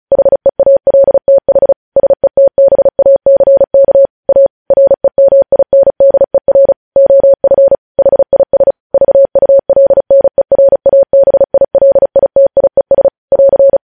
Morse code, heard as a keyed tone, about 36 wpm, 550 Hz. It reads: HEALTH SETBACK A REMINDER OF HIS VULNERABILITIES <AR>
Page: News headlines in Morse code at 35 wpm